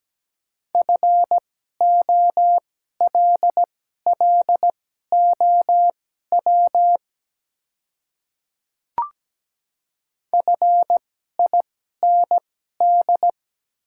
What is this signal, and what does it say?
Morse code, heard as a keyed tone, about 17 wpm, spaced as for 12 wpm, 700 Hz. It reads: FOLLOW FIND
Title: 2x Character Spacing - Top 200 Words - 17wpm